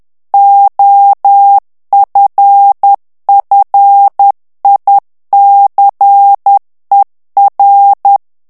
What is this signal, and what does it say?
Morse code, heard as a keyed tone, about 10 wpm, 800 Hz. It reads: OFFICER
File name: ANGLAIS texte clair 3 vitesse 10 mots minute